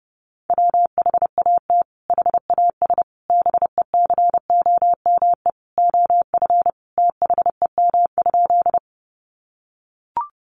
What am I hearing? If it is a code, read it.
Morse code, heard as a keyed tone, about 30 wpm, 700 Hz. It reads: WHAT HAS BECOME OF THEM?
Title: Sentences from Top 600 Words 30wpm